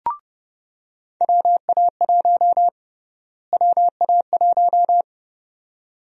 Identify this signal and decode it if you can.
Morse code, heard as a keyed tone, about 31 wpm, 700 Hz. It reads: WA1 WA1
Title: USA Call Sign Prefixes - Repeated 2x - 30wpm